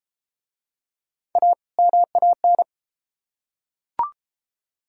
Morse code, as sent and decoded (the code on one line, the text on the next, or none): .- / -- .- -.
A MAN